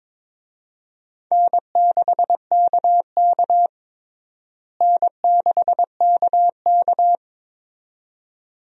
Morse code, heard as a keyed tone, about 22 wpm, 700 Hz. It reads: N6KK N6KK
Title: USA Call Signs - 4 Characters - Repeated 2x - 22wpm